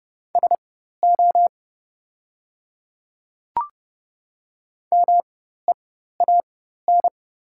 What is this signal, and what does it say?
Morse code, heard as a keyed tone, about 30 wpm, spaced as for 14 wpm, 700 Hz. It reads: SO MEAN
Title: 4x Character Spacing - Top 200 Words - 30wpm